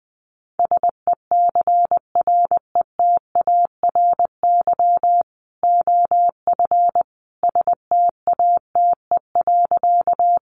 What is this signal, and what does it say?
Morse code, heard as a keyed tone, about 20 wpm, 700 Hz. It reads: SECRETARY OF STATE.
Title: Sentences from Top 1300 Words 20wpm